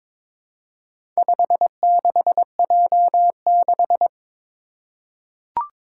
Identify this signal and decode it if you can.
Morse code, heard as a keyed tone, about 22 wpm, 700 Hz. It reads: 56J6
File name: practice / 22wpm / Four Letter-Number Combinations - 22wpm